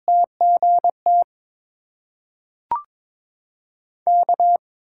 Morse code, heard as a keyed tone, about 22 wpm, 700 Hz. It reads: TGT K
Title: USA Call Signs - 6x Character Spacing - 22wpm